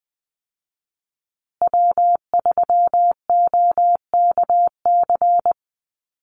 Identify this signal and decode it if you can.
Morse code, heard as a keyed tone, about 20 wpm, 700 Hz. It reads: W3OKC
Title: Call Signs Repeated 2x - 20wpm